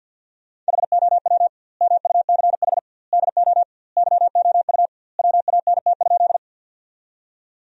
Morse code, heard as a keyed tone, about 47 wpm, 700 Hz. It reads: HOW MUCH DO YOU WANT?
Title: 2x Word-Spacing - Sentences from Top 200 Words - 50wpm